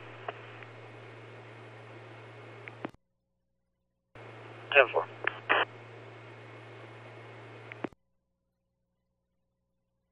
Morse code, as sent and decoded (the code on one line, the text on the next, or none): none